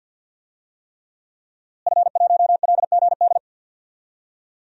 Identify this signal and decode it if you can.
Morse code, heard as a keyed tone, about 52 wpm, 700 Hz. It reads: F1LGD
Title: DXCC-100 Call Signs - 5 Characters - Repeated 3x - 50wpm